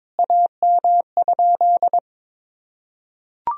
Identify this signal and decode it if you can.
Morse code, heard as a keyed tone, about 22 wpm, 700 Hz. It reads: AM?
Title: Sentences from Top 1000 Words - Review Part 2 - 22wpm